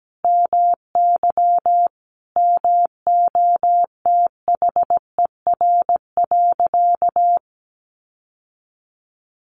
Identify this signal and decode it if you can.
Morse code, heard as a keyed tone, about 17 wpm, 700 Hz. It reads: MY MOTHER.
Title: Sentences from Top 1400 Words 17wpm